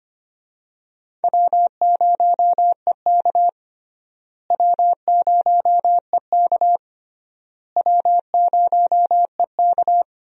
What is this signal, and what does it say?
Morse code, heard as a keyed tone, about 25 wpm, 700 Hz. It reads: W0EK W0EK W0EK
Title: USA Call Signs - 4 Characters - Repeated 3x - 25wpm